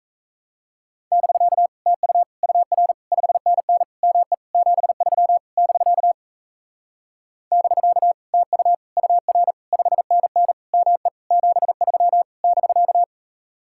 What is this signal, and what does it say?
Morse code, heard as a keyed tone, about 40 wpm, 700 Hz. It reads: <BK> TU UR 5NN ME 73 <BK> <BK> TU UR 5NN ME 73 <BK>